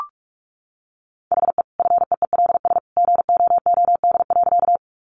Morse code, heard as a keyed tone, about 44 wpm, 700 Hz. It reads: HE FEELS GOOD.
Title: Additional Easy Phrases 45wpm